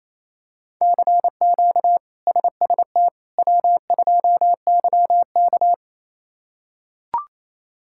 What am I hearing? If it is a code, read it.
Morse code, heard as a keyed tone, about 28 wpm, 700 Hz. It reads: CQ SST W2YK
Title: K1USN SST QSO Practice - 28wpm